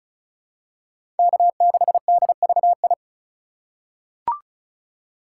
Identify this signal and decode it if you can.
Morse code, heard as a keyed tone, about 36 wpm, 700 Hz. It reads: K6DVI
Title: USA Call Signs - 5 Characters - 2x Character Spacing - 35wpm